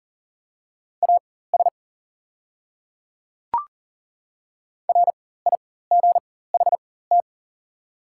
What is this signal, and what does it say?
Morse code, heard as a keyed tone, about 38 wpm, spaced as for 18 wpm, 690 Hz. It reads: AS RIGHT
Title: 4x Character Spacing - Top 200 Words - 40wpm